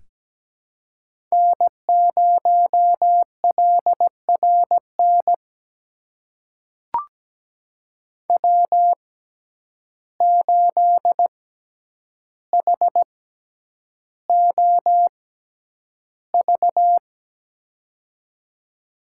Morse code, heard as a keyed tone, about 17 wpm, spaced as for 9 wpm, 700 Hz. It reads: N0LRN W8HOV
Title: USA Call Signs - 5 Characters - 6x Character Spacing - 17wpm